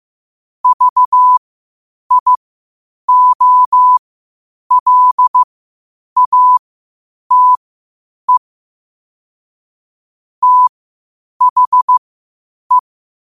Morse code, heard as a keyed tone, about 15 wpm, spaced as for 8 wpm, 1000 Hz. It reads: VIOLATE THE
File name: qotdfiles / 8wpm